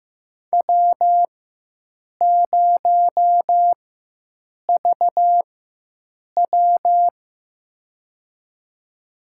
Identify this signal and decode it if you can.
Morse code, heard as a keyed tone, about 15 wpm, spaced as for 7 wpm, 700 Hz. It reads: W0VW
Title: USA Call Signs - 4 Characters - 4x Character Spacing - 15wpm